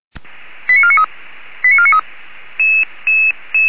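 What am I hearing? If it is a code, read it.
Five-tone selective call (ZVEI), heard as quick runs of five tones, tones of about 80 ms. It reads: 89493 89493
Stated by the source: Received by SDR, ZVEI five-tone call 89485.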